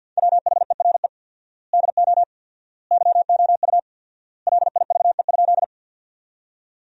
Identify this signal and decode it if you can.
Morse code, heard as a keyed tone, about 50 wpm, 700 Hz. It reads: WHERE DO YOU LIVE?
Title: 4x Word-Spacing - Sentences from Top 200 Words - 50wpm